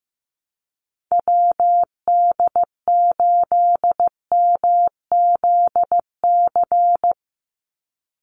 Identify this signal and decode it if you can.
Morse code, heard as a keyed tone, about 15 wpm, 700 Hz. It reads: WD8MZC
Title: USA Call Signs 15wpm